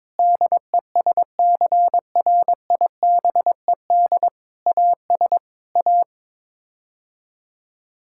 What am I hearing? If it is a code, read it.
Morse code, heard as a keyed tone, about 22 wpm, 700 Hz. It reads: DESCRIBED AS A
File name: practice / 22wpm / Sets of 7 Words - 22wpm